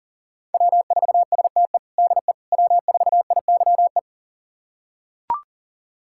Morse code, heard as a keyed tone, about 40 wpm, 700 Hz. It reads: W4STE DE W4IYE
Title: USA Callsign DE Callsign - 5 Characters - 40wpm